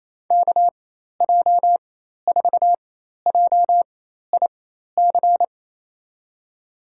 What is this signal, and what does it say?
Morse code, heard as a keyed tone, about 28 wpm, spaced as for 13 wpm, 700 Hz. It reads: KJ4JIC
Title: USA Call Signs - 6 Characters - 4x Character Spacing - 28wpm